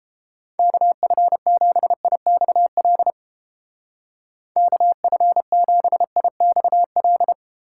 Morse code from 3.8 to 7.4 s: KF7IXL

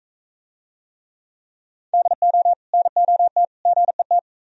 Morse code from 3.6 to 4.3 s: GET

